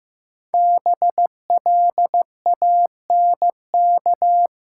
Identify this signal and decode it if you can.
Morse code, heard as a keyed tone, about 15 wpm, 700 Hz. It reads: BLANK
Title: Five Letter Words - 15wpm